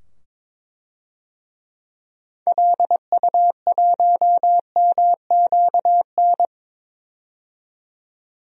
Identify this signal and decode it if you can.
Morse code, heard as a keyed tone, about 22 wpm, 700 Hz. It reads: LU1MQN
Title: DXCC-100 Call Signs - 6 Characters - 22wpm